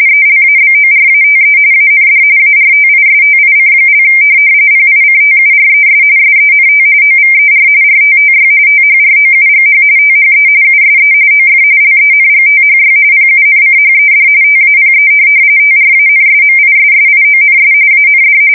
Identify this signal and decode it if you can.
RTTY, about 45.45 baud. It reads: ELABORATE AND ARTISTIC) PICTURES SENT OVER RTTY THROUGH THE USE OF LENGTHY PUNCHED TAPE TRANSMISSIONS AND
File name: fsk170 / teletype-RTTY